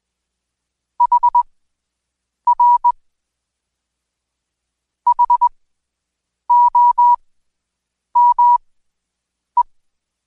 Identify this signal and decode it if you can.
Morse code, heard as a keyed tone, about 21 wpm, spaced as for 7 wpm, 970 Hz. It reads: HR HOME
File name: Code Quick / AudioDownld2 / Seg113-Day18-DayDrill 18